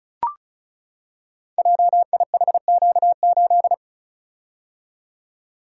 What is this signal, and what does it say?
Morse code, heard as a keyed tone, about 36 wpm, 700 Hz. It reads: JIHQ8
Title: Code Groups - 5 Letter:Numbers - 35wpm